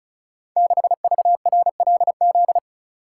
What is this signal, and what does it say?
Morse code, heard as a keyed tone, about 36 wpm, 700 Hz. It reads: 6VRLZ
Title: Code Groups - 5 Letter:Numbers - 35wpm